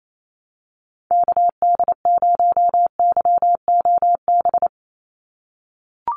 Morse code, heard as a keyed tone, about 28 wpm, 700 Hz. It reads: KD0YOB
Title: NA SOTA Activator Callsigns 28wpm